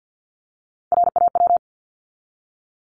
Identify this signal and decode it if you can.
Morse code, heard as a keyed tone, about 50 wpm, 700 Hz. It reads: RAW
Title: Top 1251-1500 Words 50wpm